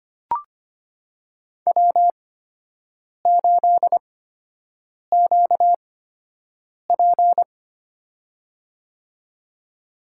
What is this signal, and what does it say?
Morse code, heard as a keyed tone, about 25 wpm, spaced as for 7 wpm, 700 Hz. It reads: W8QP
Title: USA Call Signs - 4 Characters - 8x Character Spacing - 25wpm